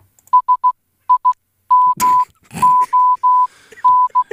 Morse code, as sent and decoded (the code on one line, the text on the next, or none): ... .. -- --- -.
SIMON